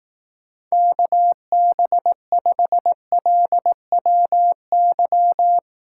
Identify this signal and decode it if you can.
Morse code, heard as a keyed tone, about 18 wpm, 700 Hz. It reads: KB5LWY
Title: Speed Racing - USA Call Signs 15wpm